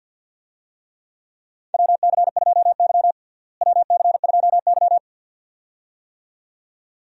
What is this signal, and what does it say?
Morse code, heard as a keyed tone, about 47 wpm, 700 Hz. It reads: WC2Y WC2Y
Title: USA Call Signs - 4 Characters - Repeated 2x - 50wpm